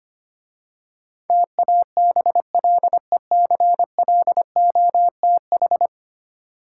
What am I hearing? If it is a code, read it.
Morse code, heard as a keyed tone, about 26 wpm, 700 Hz. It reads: TABLECLOTH
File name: Easy Words - Speed-Racing with Hints - 25wpm